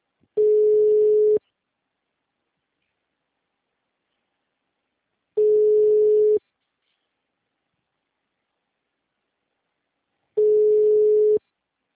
Ringback tone (Europe, 425 Hz).
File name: out- -2022- - - .49562